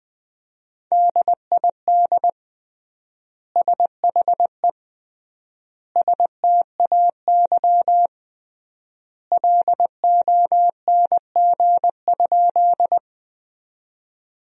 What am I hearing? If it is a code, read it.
Morse code, heard as a keyed tone, about 21 wpm, 700 Hz. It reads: DID SHE STAY LONG?
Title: 3x Word-Spacing - Sentences from Top 500 Words - 20wpm